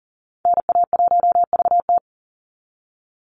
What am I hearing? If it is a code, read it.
Morse code, heard as a keyed tone, about 40 wpm, 700 Hz. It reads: NA1VT
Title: Call Signs Repeated 3x - 40wpm